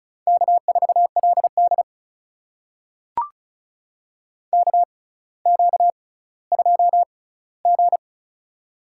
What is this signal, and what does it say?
Morse code, heard as a keyed tone, about 36 wpm, spaced as for 18 wpm, 700 Hz. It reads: K4LD KQ2G
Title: USA Call Signs - 4 Characters - 6x Character Spacing - 35wpm